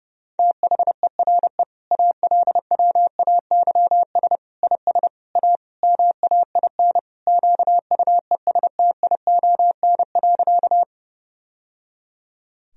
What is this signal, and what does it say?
Morse code, heard as a keyed tone, about 32 wpm, 700 Hz. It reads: THERE ALWAYS IS A MAIN QUESTION.
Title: Sentences from Top 1000 Words - Review Part 2 - 30wpm